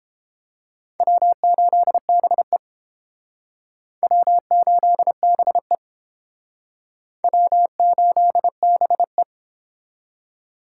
Morse code, heard as a keyed tone, about 30 wpm, 700 Hz. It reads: W8BE W8BE W8BE